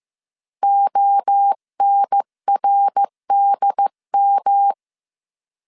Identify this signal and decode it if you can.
Morse code, heard as a keyed tone, about 15 wpm, 800 Hz. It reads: ONRDM
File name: CW / DET / 1 3 1500C S C 14wpm 2458